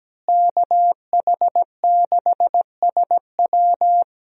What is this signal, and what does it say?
Morse code, heard as a keyed tone, about 17 wpm, 700 Hz. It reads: KH6SW
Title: DXCC-100 Call Signs - 5 Characters - Repeated 2x - 17wpm